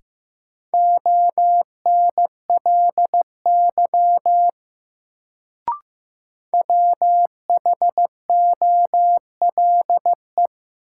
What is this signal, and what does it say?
Morse code, heard as a keyed tone, about 15 wpm, 700 Hz. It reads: ONLY WHOLE
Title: Sets of 2 Words - New Word Drill - 15wpm